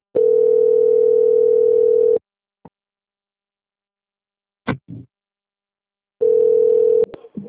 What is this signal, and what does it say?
Ringback tone (North America), 440 + 480 Hz.
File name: out- - - - - .5607